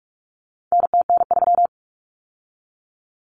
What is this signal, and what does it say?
Morse code, heard as a keyed tone, about 44 wpm, 700 Hz. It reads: NTN3
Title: Common Cut Numbers - 4-Digit 45wpm